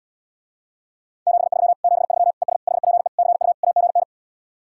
Morse code, heard as a keyed tone, about 80 wpm, 700 Hz. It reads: 73 73 ES HPE CU AGN